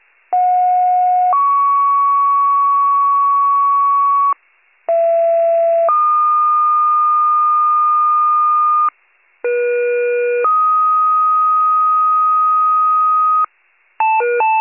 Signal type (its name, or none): two-tone page